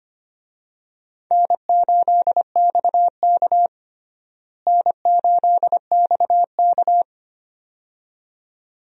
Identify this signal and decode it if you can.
Morse code, heard as a keyed tone, about 25 wpm, 700 Hz. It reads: N8XK N8XK
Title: USA Call Signs - 4 Characters - Repeated 2x - 25wpm